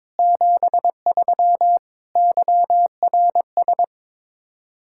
Morse code, heard as a keyed tone, about 22 wpm, 700 Hz. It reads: 73 YRS
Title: Practice QSO - Exchange 3 - More Info - 22wpm